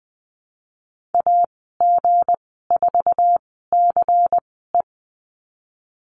Morse code, heard as a keyed tone, about 19 wpm, spaced as for 14 wpm, 700 Hz. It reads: AG4CE
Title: 173 Lesson 28g - Number 9 - Call Signs 20wpm